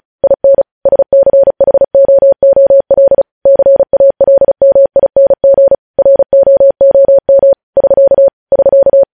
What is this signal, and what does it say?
Morse code, heard as a keyed tone, about 36 wpm, 550 Hz. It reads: IN SCHOOL CALMING ROOM <SK> <SK>
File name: podcast / News headlines-1x-35x35